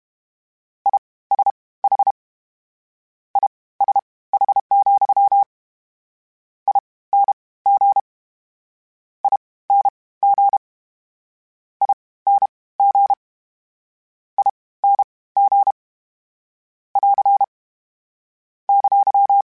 Morse code, heard as a keyed tone, about 32 wpm, spaced as for 19 wpm, 800 Hz. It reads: ISH ISH, ING ING ING ING <AR> !
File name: General / General-mp3s / Suffixes sp @30